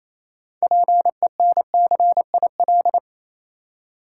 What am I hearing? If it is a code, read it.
Morse code, heard as a keyed tone, about 27 wpm, 690 Hz. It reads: PENCIL
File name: Easy Words - Speed-Racing with Hints - 22wpm